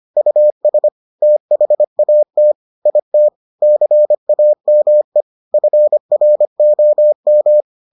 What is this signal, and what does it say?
Morse code, heard as a keyed tone, about 25 wpm, 600 Hz. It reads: US THAT IT CAME FROM